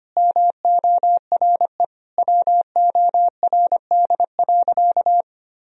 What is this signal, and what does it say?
Morse code, heard as a keyed tone, about 25 wpm, 700 Hz. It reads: MORE WORD.
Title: Sentences from Top 1000 Words - Review Part 1 - 25wpm